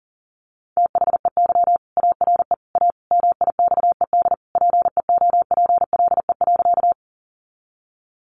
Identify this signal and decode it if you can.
Morse code, heard as a keyed tone, about 40 wpm, 700 Hz. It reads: THEY ARE A MIXED PEOPLE.
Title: Sentences from Top 1500 Words 40wpm